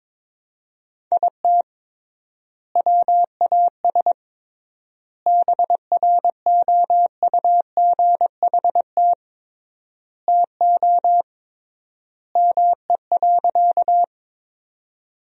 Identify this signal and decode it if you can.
Morse code, heard as a keyed tone, about 22 wpm, 700 Hz. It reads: IT WAS BROUGHT TO ME.
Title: 3x Word-Spacing - Sentences from Top 500 Words - 22wpm